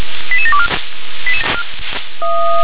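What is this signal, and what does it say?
Five-tone selective call (ZVEI), heard as quick runs of five tones, tones of about 70 ms. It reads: 88724
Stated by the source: ZVEI five-tone call 88709 received by SDR.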